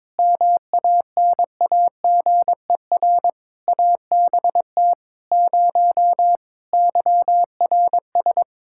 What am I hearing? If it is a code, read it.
Morse code, heard as a keyed tone, about 22 wpm, 700 Hz. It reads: MANAGER ABT 0 YRS
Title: Practice QSO - Exchange 3 - More Info - 22wpm